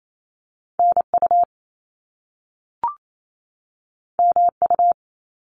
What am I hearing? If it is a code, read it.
Morse code, heard as a keyed tone, about 28 wpm, 700 Hz. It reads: NU MU
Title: Top Two Letter Combinations in Words 28wpm